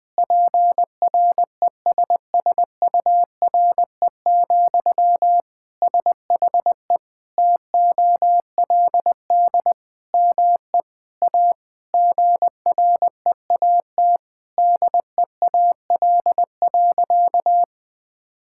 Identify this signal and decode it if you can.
Morse code, heard as a keyed tone, about 20 wpm, 700 Hz. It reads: PRESSURE, SHE TOLD ME A GREAT DEAL.
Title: Sentences from Top 1000 Words - Review Part 6 - 20wpm